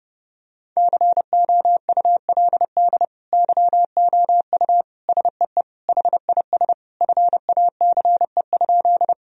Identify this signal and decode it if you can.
Morse code, heard as a keyed tone, about 32 wpm, 700 Hz. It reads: COULD YOU SEE HIS FACE?
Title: Sentences from Top 1000 Words - Review Part 2 - 30wpm